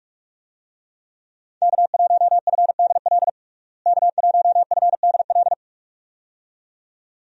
Morse code, heard as a keyed tone, about 41 wpm, 700 Hz. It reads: K1FDL K1FDL
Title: USA Call Signs - 5 Characters - Repeated 2x - 45wpm